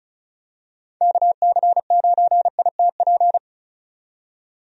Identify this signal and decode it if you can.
Morse code, heard as a keyed tone, about 36 wpm, 700 Hz. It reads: KC9ITP
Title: USA Call Signs - 6 Characters - 35wpm